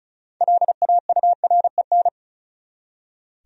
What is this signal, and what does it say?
Morse code, heard as a keyed tone, about 34 wpm, 700 Hz. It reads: LAUREN
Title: USA Common Names - 35wpm